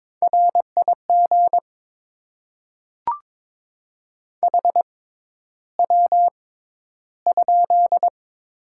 Morse code, heard as a keyed tone, about 22 wpm, spaced as for 11 wpm, 700 Hz. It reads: RIG HW?
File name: Top 20 QSO Elements - 6x Character Spacing - 22wpm